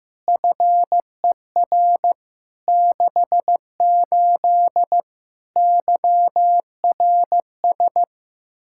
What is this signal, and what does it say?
Morse code, heard as a keyed tone, about 15 wpm, 700 Hz. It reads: FER 68 YRS
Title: Practice QSO - Exchange 3 - More Info - 15wpm